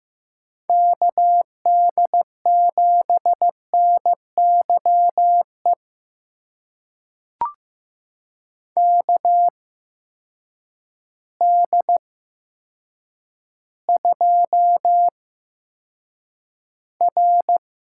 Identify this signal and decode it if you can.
Morse code, heard as a keyed tone, about 15 wpm, spaced as for 8 wpm, 700 Hz. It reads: KD7NYE KD2R
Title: USA Call Signs - 6 Characters - 8x Character Spacing - 15wpm